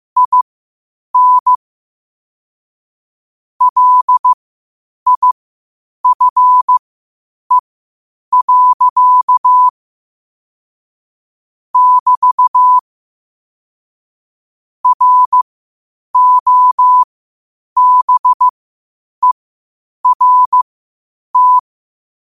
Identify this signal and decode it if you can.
Morse code, heard as a keyed tone, about 15 wpm, spaced as for 8 wpm, 1000 Hz. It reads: IN LIFE. <BT> ROBERT
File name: qotdfiles / 8wpm